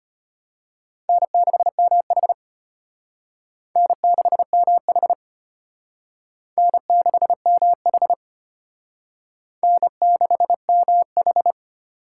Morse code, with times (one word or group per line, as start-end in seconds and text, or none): 1.1-2.3 N6MH
3.8-5.1 N6MH
6.6-8.1 N6MH
9.6-11.5 N6MH